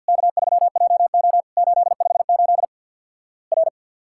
Morse code, heard as a keyed tone, about 51 wpm, 700 Hz. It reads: K3JO 858 R
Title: Worked All Europe DX Contest - QTC Exchange with Repeats - 50wpm